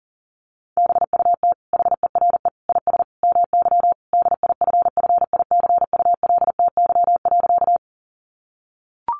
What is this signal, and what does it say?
Morse code, heard as a keyed tone, about 40 wpm, 700 Hz. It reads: BUT HERE IS MY DIFFICULTY.
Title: Sentences from Top 1500 Words 40wpm